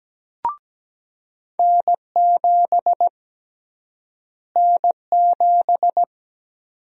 Morse code, heard as a keyed tone, about 17 wpm, 700 Hz. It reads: N7 N7
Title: USA Call Sign Prefixes - Repeated 2x - 17wpm